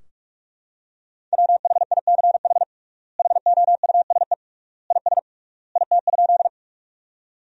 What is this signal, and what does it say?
Morse code, heard as a keyed tone, about 46 wpm, 700 Hz. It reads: WHICH HOUSE IS IT?
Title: 3x Word-Spacing - Sentences from Top 200 Words - 45wpm